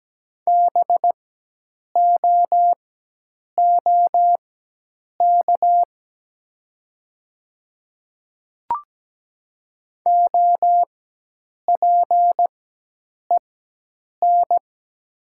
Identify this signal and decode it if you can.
Morse code, heard as a keyed tone, about 17 wpm, spaced as for 8 wpm, 700 Hz. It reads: BOOK OPEN
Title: 4x Character Spacing - Top 300 Words - 17wpm